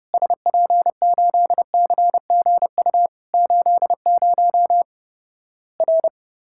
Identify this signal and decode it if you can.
Morse code, heard as a keyed tone, about 30 wpm, 700 Hz. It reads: SP8CGU 80 R
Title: Worked All Europe DX Contest - QTC Exchange - 30wpm